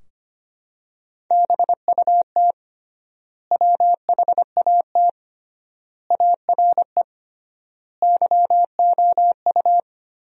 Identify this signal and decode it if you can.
Morse code, heard as a keyed tone, about 25 wpm, 700 Hz. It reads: BUT WHAT ARE YOU
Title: 3x Word-Spacing - Sentences from Top 100 Words - 25wpm